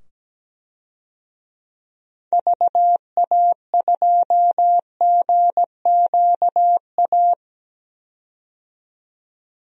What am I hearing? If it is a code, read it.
Morse code, heard as a keyed tone, about 17 wpm, 700 Hz. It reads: VA2GQA
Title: DXCC-100 Call Signs - 6 Characters - 17wpm